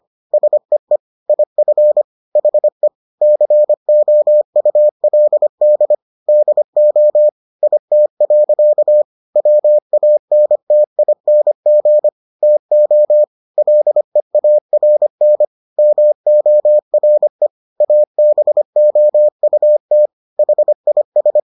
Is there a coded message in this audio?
Yes: Morse code from 0.3 to 21.4 s, heard as a keyed tone, about 25 wpm, 600 Hz. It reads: SEE IF HE COULD DO IT. WANTING TO LEARN MORE ABOUT HIS